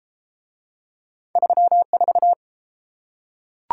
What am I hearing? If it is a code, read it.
Morse code, heard as a keyed tone, about 31 wpm, 700 Hz. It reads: 34